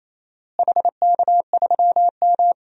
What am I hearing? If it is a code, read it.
Morse code, heard as a keyed tone, about 28 wpm, 700 Hz. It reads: HK3M
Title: DXCC-100 Call Signs - 4 Characters - Repeated 3x - 28wpm